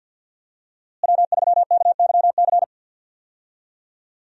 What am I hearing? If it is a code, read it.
Morse code, heard as a keyed tone, about 50 wpm, 700 Hz. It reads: W3KYC
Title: USA Call Signs - 5 Characters - Repeated 2x - 50wpm